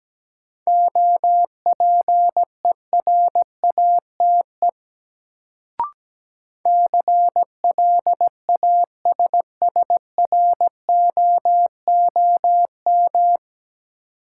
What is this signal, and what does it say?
Morse code, heard as a keyed tone, about 17 wpm, 700 Hz. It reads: OPERATE CLASSROOM
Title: Sentences from Top 1500 Words - New Word Drill - 17wpm